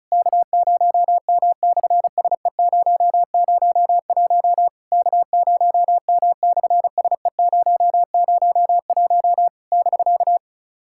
Morse code, heard as a keyed tone, about 36 wpm, 700 Hz. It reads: K0M/SE001 K0M/SE001 <BK>
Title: SOTA QSO Practice 35wpm